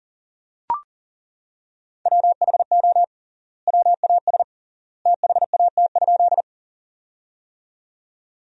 Morse code, heard as a keyed tone, about 40 wpm, 700 Hz. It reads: WHO WAS THAT?
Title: 3x Word-Spacing - Sentences from Top 100 Words - 40wpm